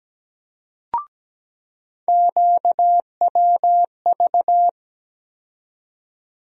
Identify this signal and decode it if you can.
Morse code, heard as a keyed tone, about 17 wpm, 700 Hz. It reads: QWV
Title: USA Call Sign Postfixes - 3 Letters 17wpm